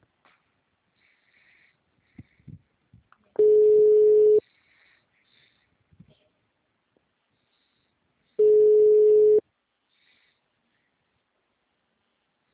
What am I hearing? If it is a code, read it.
Ringback tone (Europe, 425 Hz).
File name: out- -2011- - - .68311